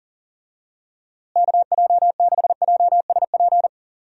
Morse code, heard as a keyed tone, about 40 wpm, 700 Hz. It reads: KJ6JSP
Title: USA Call Signs - 6 Characters - Repeated 2x - 40wpm